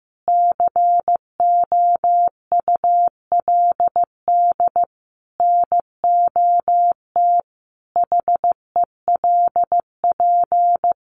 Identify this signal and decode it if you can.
Morse code, heard as a keyed tone, about 15 wpm, 700 Hz. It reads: COULD NOT HELP
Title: Sentences from Top 600 Words 15wpm